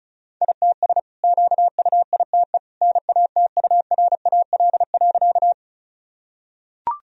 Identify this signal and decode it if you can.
Morse code, heard as a keyed tone, about 36 wpm, 700 Hz. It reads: ITS QUITE NATURAL.
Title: Sentences from Top 1000 Words - Review Part 4 - 35wpm